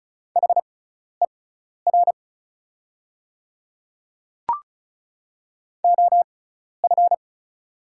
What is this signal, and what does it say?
Morse code, heard as a keyed tone, about 36 wpm, spaced as for 12 wpm, 700 Hz. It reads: HER OF
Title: 6x Character Spacing - Top 100 Words - 35wpm